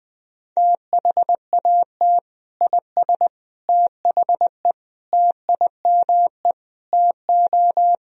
Morse code, heard as a keyed tone, about 20 wpm, 700 Hz. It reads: THAT IS THE TIME TO
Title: Sentences from Top 1000 Words - Review Part 1 - 20wpm